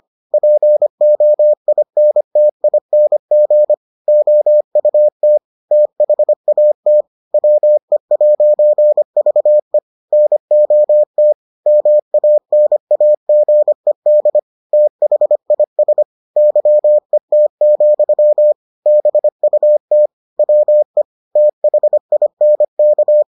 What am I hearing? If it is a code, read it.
Morse code, heard as a keyed tone, about 25 wpm, 600 Hz. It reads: POINTING OUT THAT WE'VE NOT MANAGED THIS YET, BUT WE THINK